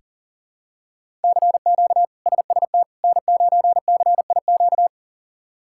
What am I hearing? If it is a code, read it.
Morse code, heard as a keyed tone, about 40 wpm, 700 Hz. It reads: CQ SST N9CIQ
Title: K1USN SST QSO Practice - 40wpm